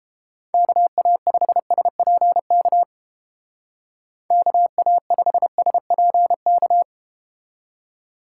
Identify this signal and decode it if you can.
Morse code, heard as a keyed tone, about 32 wpm, 700 Hz. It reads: KA5SPK KA5SPK